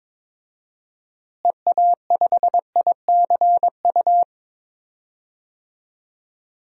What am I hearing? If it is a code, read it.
Morse code, heard as a keyed tone, about 21 wpm, 700 Hz. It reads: EA5ICU
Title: DXCC-100 Call Signs - 6 Characters - Repeated 3x - 22wpm